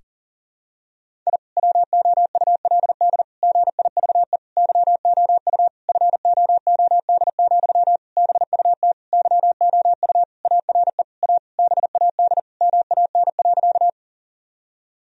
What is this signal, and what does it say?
Morse code, heard as a keyed tone, about 40 wpm, 700 Hz. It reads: I WOULD GIVE YOU FOOD, BUT YOU ARE A BAD MAN.
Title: Sentences from Top 1000 Words - Review Part 1 - 40wpm